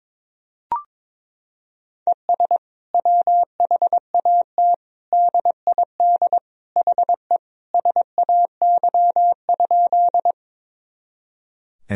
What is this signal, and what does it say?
Morse code, heard as a keyed tone, about 22 wpm, 700 Hz. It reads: ES WHAT DID HE SAY?
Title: Sentences from Top 200 Words and Abbreviations 22wpm